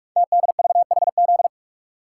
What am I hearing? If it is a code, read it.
Morse code, heard as a keyed tone, about 44 wpm, 690 Hz. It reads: TDVHZ
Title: Code Groups - 5 Letters - 45wpm